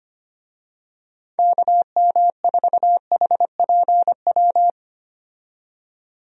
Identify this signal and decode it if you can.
Morse code, heard as a keyed tone, about 25 wpm, 700 Hz. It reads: KM4HPW
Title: USA Call Signs - 6 Characters - 25wpm